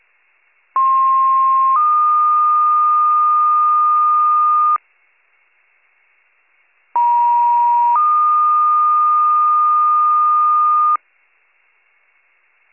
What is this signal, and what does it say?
Two-tone page, heard as a one-second tone, then a three-second tone, about 1060 then 1220 Hz, about 950 then 1220 Hz. It is heard from 0.8 to 11.0 s.